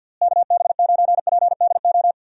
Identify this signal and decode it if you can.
Morse code, heard as a keyed tone, about 50 wpm, 700 Hz. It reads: KB9PDO
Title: USA Call Signs - 6 Characters - Repeated 3x - 50wpm